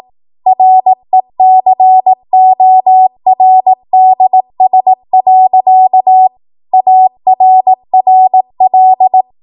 Morse code, heard as a keyed tone, about 18 wpm, 750 Hz. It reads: RECORDS. ARRL
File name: Morse / 18WPM